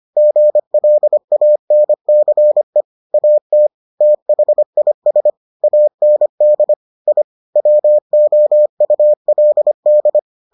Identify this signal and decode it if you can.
Morse code, heard as a keyed tone, about 25 wpm, 600 Hz. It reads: GLANCE AT THIS AND I WOULD